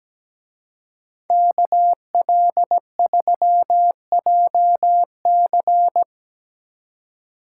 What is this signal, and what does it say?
Morse code, heard as a keyed tone, about 19 wpm, 700 Hz. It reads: KL3JC
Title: Speed Racing - USA 2x2 Call Signs 17wpm